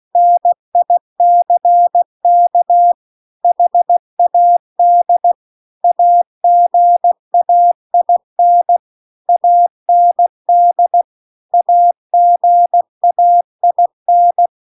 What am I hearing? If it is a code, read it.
Morse code, heard as a keyed tone, about 16 wpm, 700 Hz. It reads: NICK HAD AGAIN AND AGAIN
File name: CW-The-Big-Twoheaded-River-2-16WPM